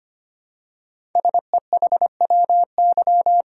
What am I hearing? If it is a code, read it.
Morse code, heard as a keyed tone, about 25 wpm, 700 Hz. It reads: SEHWY
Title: Code Groups - 5 Letters - 25wpm